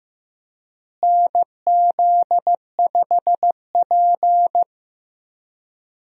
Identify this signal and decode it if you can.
Morse code, heard as a keyed tone, about 15 wpm, 700 Hz. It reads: NZ5P
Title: USA Call Signs - 4 Characters - Repeated 2x - 15wpm